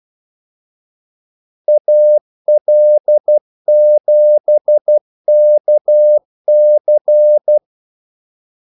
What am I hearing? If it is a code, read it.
Morse code, heard as a keyed tone, about 12 wpm, 600 Hz. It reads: AL7KC
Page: 2 ekstra ordmellemrum mellem kaldesignalerne